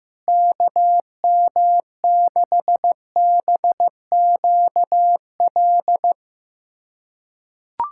Morse code, heard as a keyed tone, about 15 wpm, 700 Hz. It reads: KM6BQL
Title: USA Callsign DE Callsign - 6 Characters - 15wpm